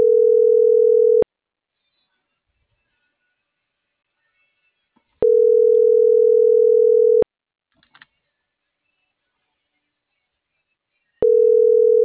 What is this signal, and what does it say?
Ringback tone (North America), 440 + 480 Hz.